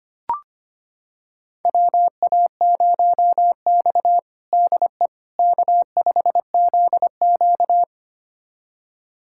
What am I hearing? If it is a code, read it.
Morse code, heard as a keyed tone, about 25 wpm, 700 Hz. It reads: WA0X DE K5ZQ
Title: USA Callsign DE Callsign - 4 Characters - 25wpm